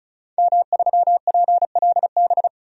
Morse code, heard as a keyed tone, about 36 wpm, 700 Hz. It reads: M3PLB
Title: DXCC-100 Call Signs - 5 Characters - 35wpm